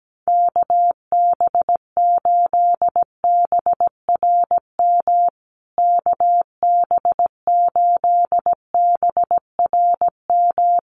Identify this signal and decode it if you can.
Morse code, heard as a keyed tone, about 17 wpm, 700 Hz. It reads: KB8BRM KB8BRM
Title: USA Call Signs Repeated 2x 17wpm